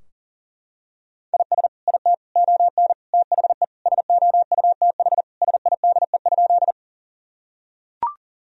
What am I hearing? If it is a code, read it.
Morse code, heard as a keyed tone, about 40 wpm, 700 Hz. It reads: IS IT ON THE SOUTH SIDE?
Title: Sentences from Top 1000 Words - Review Part 2 - 40wpm